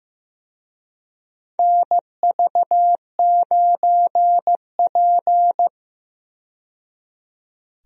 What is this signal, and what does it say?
Morse code, heard as a keyed tone, about 15 wpm, 700 Hz. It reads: NV9P
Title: USA Call Signs - 4 Characters - 15wpm